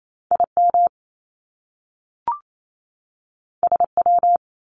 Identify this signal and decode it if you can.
Morse code, heard as a keyed tone, about 28 wpm, 700 Hz. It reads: IM SW
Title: Top Two Letter Combinations in Words 28wpm